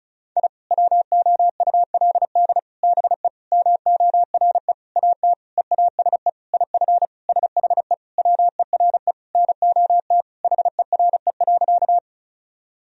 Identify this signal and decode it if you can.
Morse code, heard as a keyed tone, about 34 wpm, 700 Hz. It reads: I WOULD BE MORE AT EASE IF SHE WERE NOT HERE.
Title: Sentences from Top 1000 Words - Review Part 1 - 35wpm